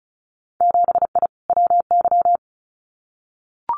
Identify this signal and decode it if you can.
Morse code, heard as a keyed tone, about 36 wpm, 700 Hz. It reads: 7I WY
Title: Winter Field Day Exchanges Encore 35wpm